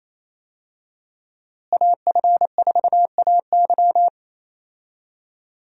Morse code, heard as a keyed tone, about 28 wpm, 700 Hz. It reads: AF4AY
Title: USA Call Signs - 5 Characters - Repeated 3x - 28wpm